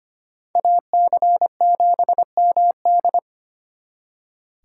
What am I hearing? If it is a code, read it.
Morse code, heard as a keyed tone, about 33 wpm, 700 Hz. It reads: AC7MD